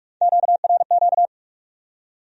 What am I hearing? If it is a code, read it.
Morse code, heard as a keyed tone, about 58 wpm, 700 Hz. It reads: QRQ